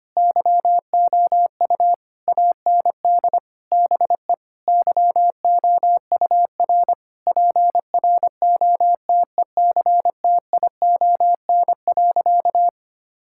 Morse code, heard as a keyed tone, about 25 wpm, 700 Hz. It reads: YOU AND BE YOUR PROTECTION.
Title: Sentences from Top 1600 Words 25wpm